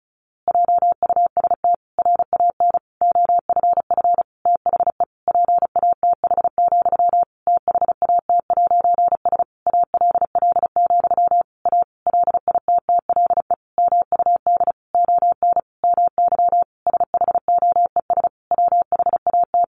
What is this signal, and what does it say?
Morse code, heard as a keyed tone, about 34 wpm, 700 Hz. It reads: JUST RAN OFF THE PATH, THAT'S ALL, A LITTLE MUD ON MY SHOES WHAT
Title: A Sound Of Thunder 35wpm